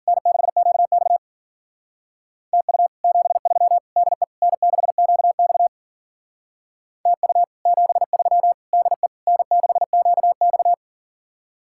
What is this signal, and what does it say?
Morse code, heard as a keyed tone, about 48 wpm, 700 Hz. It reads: N6QX TU 73 DE N6QX TU 73 DE N6QX
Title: Speed-Racing Common QSO Phrases - 40wpm